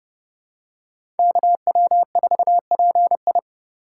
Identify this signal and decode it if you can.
Morse code, heard as a keyed tone, about 30 wpm, 700 Hz. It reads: KW4PI